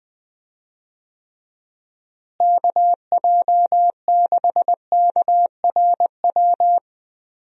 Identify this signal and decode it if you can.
Morse code, heard as a keyed tone, about 20 wpm, 700 Hz. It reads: KJ6KRW
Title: USA Call Signs - 6 Characters - Repeated 3x - 20wpm